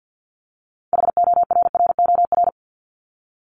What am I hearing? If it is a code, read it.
Morse code, heard as a keyed tone, about 50 wpm, 700 Hz. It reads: HORROR
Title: Top 1501-1750 Words 50wpm